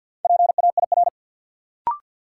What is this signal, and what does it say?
Morse code, heard as a keyed tone, about 53 wpm, 700 Hz. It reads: PAIR
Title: Sentences from Top 1200 Words - New Word Drill Alt - 50wpm